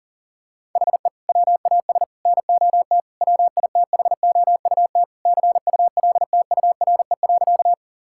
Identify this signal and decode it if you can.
Morse code, heard as a keyed tone, about 40 wpm, 700 Hz. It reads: HE WAS NOT WITHOUT CULTURE.
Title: Sentences from Top 1000 Words - Review Part 6 - 40wpm